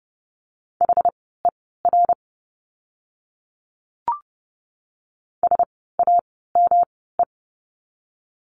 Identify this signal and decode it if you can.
Morse code, heard as a keyed tone, about 30 wpm, spaced as for 17 wpm, 700 Hz. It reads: HER SAME
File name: course / 30:10wpm / 230 Lesson 37f - Letter x - Words 30wpm